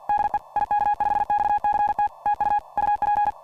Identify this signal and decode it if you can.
Morse code, heard as a keyed tone, about 41 wpm, 830 Hz. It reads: DE IK5XCT TU UP
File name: ik5xct / dxkey / dxkey rev4 hs